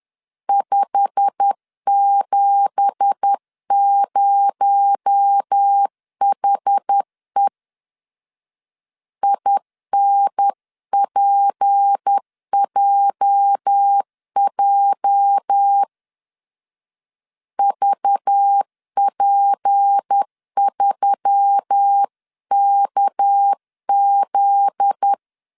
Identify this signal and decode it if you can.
Morse code, heard as a keyed tone, about 10 wpm, 800 Hz. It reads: 570HE INPJJ VP3KZ
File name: CW / DET / 1 9 1500C S A 10wpm 4511